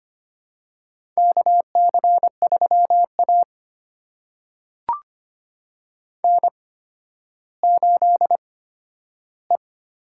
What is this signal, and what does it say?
Morse code, heard as a keyed tone, about 25 wpm, spaced as for 12 wpm, 700 Hz. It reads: KC3A N8E
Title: USA Call Signs - 8x Character Spacing - 25wpm